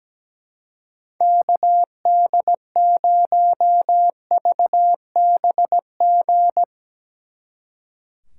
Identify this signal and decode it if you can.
Morse code, heard as a keyed tone, about 18 wpm, 700 Hz. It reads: KD0VBG
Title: Speed Racing - USA 2x3 Call Signs 17wpm